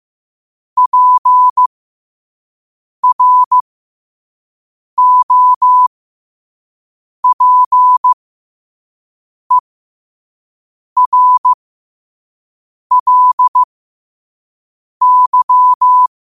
Morse code, heard as a keyed tone, about 15 wpm, spaced as for 5 wpm, 1000 Hz. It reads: PROPERLY
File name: qotdfiles / 5wpm